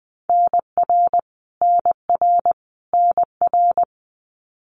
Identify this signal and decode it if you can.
Morse code, heard as a keyed tone, about 20 wpm, 700 Hz. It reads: NR NR NR
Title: Abbreviations Repeated 3x - 20wpm